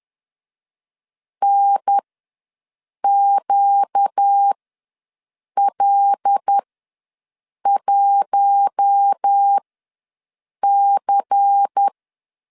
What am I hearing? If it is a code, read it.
Morse code, heard as a keyed tone, about 10 wpm, spaced as for 6 wpm, 800 Hz. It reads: NQL1C
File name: LEZIONI / 9 2 ws 10wpm 4130